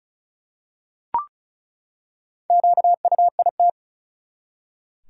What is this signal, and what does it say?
Morse code, heard as a keyed tone, about 36 wpm, 700 Hz. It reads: QUIT
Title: Four Letter Words - 35wpm